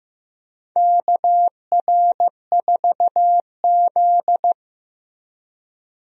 Morse code, heard as a keyed tone, about 15 wpm, 700 Hz. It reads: KR4Z
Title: USA Call Signs - 4 Characters - Repeated 3x - 15wpm